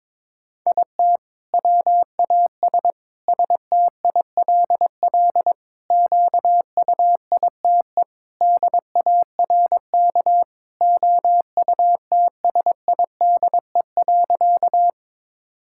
Morse code, heard as a keyed tone, about 22 wpm, 700 Hz. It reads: IT WAS STILL QUITE DARK OUTSIDE.
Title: Sentences from Top 1000 Words - Review Part 6 - 22wpm